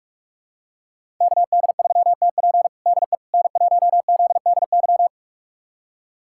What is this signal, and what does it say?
Morse code, heard as a keyed tone, about 44 wpm, 700 Hz. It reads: KD3TP DE N1ZDY
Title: USA Callsign DE Callsign - 5 Characters - 45wpm